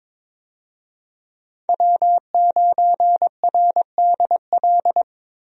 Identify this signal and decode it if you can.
Morse code, heard as a keyed tone, about 22 wpm, 700 Hz. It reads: W9RDL
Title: USA Call Signs - 5 Characters - Repeated 2x - 22wpm